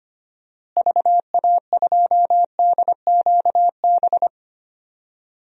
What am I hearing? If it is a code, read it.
Morse code, heard as a keyed tone, about 25 wpm, 700 Hz. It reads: VA2DQB
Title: DXCC-100 Call Signs - 6 Characters - Repeated 3x - 25wpm